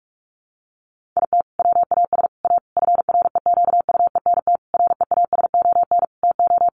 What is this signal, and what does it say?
Morse code, heard as a keyed tone, about 44 wpm, 700 Hz. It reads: IT WAS A FREQUENT REASON TO
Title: Sentences from Top 1400 Words 45wpm